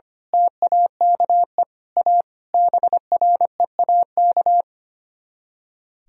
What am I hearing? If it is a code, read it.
Morse code, heard as a keyed tone, about 24 wpm, 700 Hz. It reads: TAKE A BREAK
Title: Speed-Racing - Sets of 5 Words 25wpm